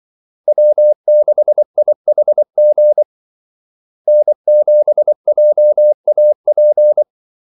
Morse code, heard as a keyed tone, about 24 wpm, 600 Hz. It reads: W6IHG N7JAP
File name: Callsigns-24wpm